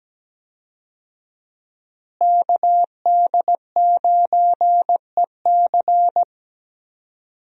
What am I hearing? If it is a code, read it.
Morse code, heard as a keyed tone, about 17 wpm, 700 Hz. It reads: KD9EC
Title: USA Call Signs - 5 Characters - Repeated 2x - 17wpm